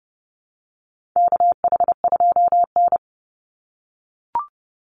Morse code, heard as a keyed tone, about 30 wpm, 700 Hz. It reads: KH2N
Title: Call Signs - 30wpm